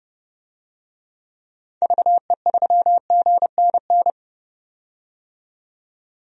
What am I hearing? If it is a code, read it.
Morse code, heard as a keyed tone, about 31 wpm, 700 Hz. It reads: VE3GNN
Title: DXCC-100 Call Signs - 6 Characters - 30wpm